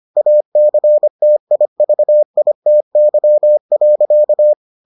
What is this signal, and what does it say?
Morse code, heard as a keyed tone, about 25 wpm, 600 Hz. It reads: ACTIVITY.